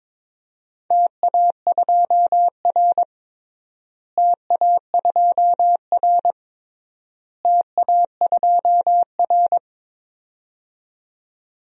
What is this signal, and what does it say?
Morse code, heard as a keyed tone, about 22 wpm, 700 Hz. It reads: TA2R TA2R TA2R
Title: DXCC-100 Call Signs - 4 Characters - Repeated 3x - 22wpm